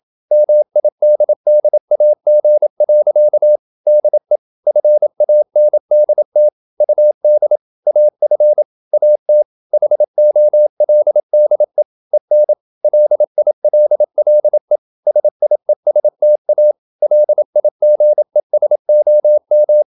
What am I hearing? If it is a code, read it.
Morse code, heard as a keyed tone, about 27 wpm, 600 Hz. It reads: MIDDAG. DE FANDT UD AF AT HOLDE EN LILLE SIESTA LIGESOM
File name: Ferie QRQ 27wpm